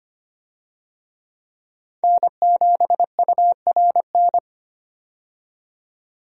Morse code, heard as a keyed tone, about 25 wpm, 700 Hz. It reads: N7URN
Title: USA Call Signs - 5 Characters - 25wpm